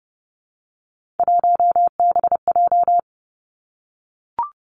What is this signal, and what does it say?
Morse code, heard as a keyed tone, about 30 wpm, 700 Hz. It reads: 1BJ
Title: Three Challenging Letter-Number Combinations 30wpm